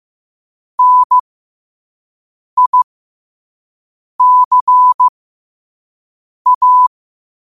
Morse code, heard as a keyed tone, about 15 wpm, spaced as for 5 wpm, 1000 Hz. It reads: NICA
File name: qotdfiles / 5wpm